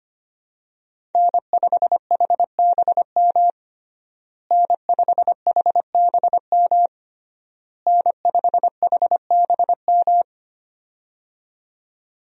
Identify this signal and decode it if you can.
Morse code, heard as a keyed tone, about 25 wpm, 700 Hz. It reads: N5HBM N5HBM N5HBM
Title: USA Call Signs - 5 Characters - Repeated 3x - 25wpm